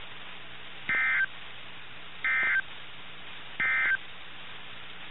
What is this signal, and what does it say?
Emergency Alert System (SAME) tones, heard as harsh buzzing bursts. It is heard from 0.8 to 4.0 s.
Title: NOAA Weather Radio - KIH61